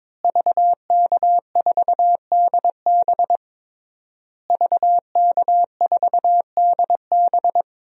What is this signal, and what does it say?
Morse code, heard as a keyed tone, about 21 wpm, 700 Hz. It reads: VK4DB VK4DB
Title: DXCC-100 Call Signs - 5 Characters - Repeated 3x - 22wpm